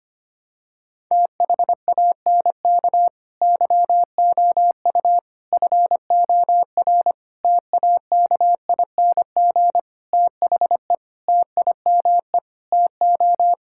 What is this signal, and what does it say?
Morse code, heard as a keyed tone, about 25 wpm, 700 Hz. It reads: THANK YOU FOR TAKING THE TIME TO
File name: practice / 25wpm / Sentences with 9-Word Phrases - 25wpm